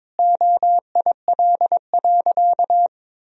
Morse code, heard as a keyed tone, about 22 wpm, 700 Hz. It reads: OIL.
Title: Sentences from Top 1000 Words - Review Part 4 - 22wpm